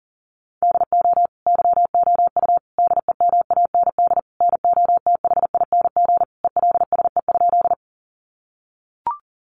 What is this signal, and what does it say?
Morse code, heard as a keyed tone, about 40 wpm, 700 Hz. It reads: DO YOU DEMAND NOTHING ELSE?
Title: Sentences from Top 800 Words 40wpm